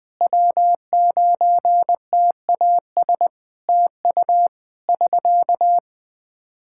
Morse code, heard as a keyed tone, about 20 wpm, 700 Hz. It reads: W9TAS TU <SK>
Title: Practice QSO - Exchange 4 - Ending - 20wpm